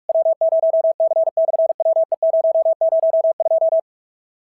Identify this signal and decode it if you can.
Morse code, heard as a keyed tone, about 46 wpm, 690 Hz. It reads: W0C/WE002
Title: SOTA QSO Practice 45wpm